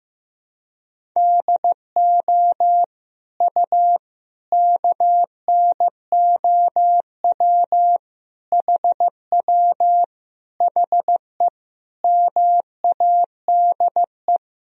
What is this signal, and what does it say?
Morse code, heard as a keyed tone, about 15 wpm, 700 Hz. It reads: DO U KNOW HW HE MADE
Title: Sentences from Top 300 Words and Abbreviations 15wpm